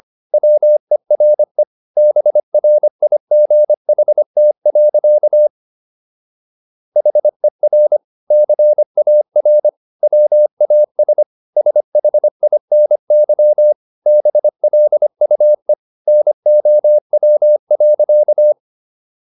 Morse code, heard as a keyed tone, about 25 wpm, 600 Hz. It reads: WERE BRIGHT. HER CAR WAS SHINY BLUE NOW.